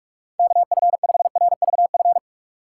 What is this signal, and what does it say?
Morse code, heard as a keyed tone, about 47 wpm, 700 Hz. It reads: KF5RVF
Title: USA Call Signs - 6 Characters - 6x Character Spacing - 45wpm